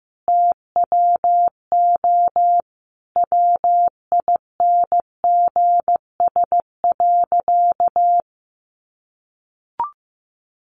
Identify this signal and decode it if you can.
Morse code, heard as a keyed tone, about 15 wpm, 700 Hz. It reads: TWO WINGS.
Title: Additional Easy Phrases 15wpm